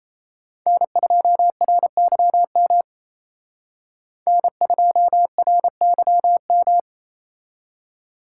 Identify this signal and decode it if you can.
Morse code, heard as a keyed tone, about 34 wpm, 700 Hz. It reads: N2RYM N2RYM
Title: Speed Racing - USA 1x3 Call Signs 28wpm